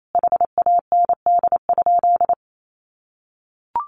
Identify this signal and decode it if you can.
Morse code, heard as a keyed tone, about 28 wpm, 700 Hz. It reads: HAND?
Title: Sentences from Top 200 Words 28wpm